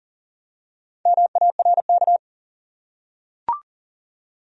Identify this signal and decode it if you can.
Morse code, heard as a keyed tone, about 40 wpm, 700 Hz. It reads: MARK
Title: USA Common Names - Max 4 Letters - 40wpm